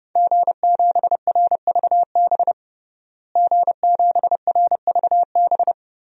Morse code, heard as a keyed tone, about 30 wpm, 700 Hz. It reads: G7RVB G7RVB
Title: DXCC-100 Call Signs - 5 Characters - Repeated 2x - 30wpm